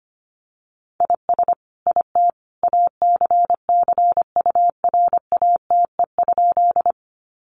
Morse code, heard as a keyed tone, about 25 wpm, 700 Hz. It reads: IS IT ACCURATE?
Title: Sentences from Top 1200 Words 25wpm